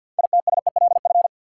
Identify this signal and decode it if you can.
Morse code, heard as a keyed tone, about 49 wpm, 700 Hz. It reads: ITSELF
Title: Sentences from Top 800 Words - New Word Drill - 50wpm